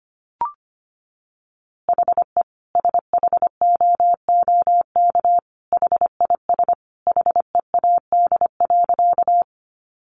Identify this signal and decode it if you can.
Morse code, heard as a keyed tone, about 25 wpm, 700 Hz. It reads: HE SHOOK HIS HEAD.
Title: A Sound Of Thunder 25wpm